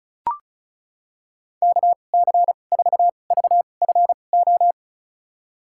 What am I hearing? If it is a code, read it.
Morse code, heard as a keyed tone, about 36 wpm, spaced as for 25 wpm, 700 Hz. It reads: KC4VFO
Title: USA Call Signs - 6 Characters - 2x Character Spacing - 35wpm